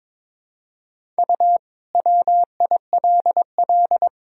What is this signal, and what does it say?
Morse code, heard as a keyed tone, about 22 wpm, 700 Hz. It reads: U WILL
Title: Sentences from Top 400 Words and Abbreviations 22wpm